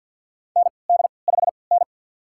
Morse code, heard as a keyed tone, about 46 wpm, spaced as for 27 wpm, 700 Hz. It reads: ND5N
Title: USA Call Signs - 4 Characters - 3x Character Spacing - 50wpm